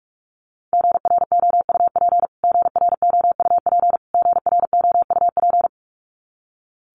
Morse code, heard as a keyed tone, about 44 wpm, 700 Hz. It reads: GROUP GROUP GROUP
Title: Top 500 Words Repeated 45wpm